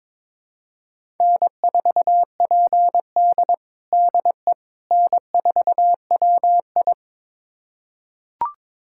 Morse code, heard as a keyed tone, about 22 wpm, 700 Hz. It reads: N4PD DE N4WI
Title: USA Callsign DE Callsign - 4 Characters - 22wpm